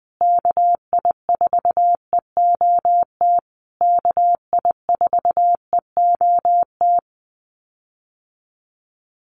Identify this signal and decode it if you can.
Morse code, heard as a keyed tone, about 20 wpm, 700 Hz. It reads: KI4EOT KI4EOT
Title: USA Call Signs Repeated 3x 20wpm